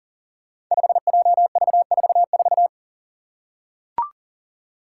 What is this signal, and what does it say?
Morse code, heard as a keyed tone, about 40 wpm, 700 Hz. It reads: 5JV44
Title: Five Challenging Letter-Number Combinations 40wpm